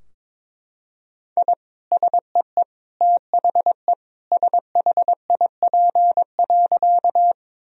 Morse code, heard as a keyed tone, about 22 wpm, 700 Hz. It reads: I SEE THE SHIP.
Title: Sentences from Top 1000 Words - Review Part 2 - 22wpm